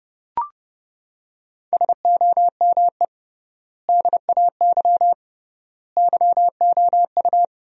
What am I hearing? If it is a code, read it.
Morse code, heard as a keyed tone, about 30 wpm, 700 Hz. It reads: SOME DAY YOU
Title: 3x Word-Spacing - Sentences from Top 200 Words - 30wpm